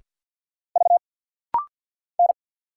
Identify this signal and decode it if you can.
Morse code, heard as a keyed tone, about 50 wpm, 700 Hz. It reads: V N
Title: Single Letter-Number - Rapid-Fire 50wpm